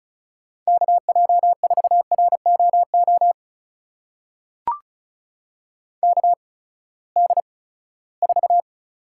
Morse code, heard as a keyed tone, about 35 wpm, spaced as for 20 wpm, 700 Hz. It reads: KJ4ROO KD4
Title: USA Call Signs - 6 Characters - 8x Character Spacing - 35wpm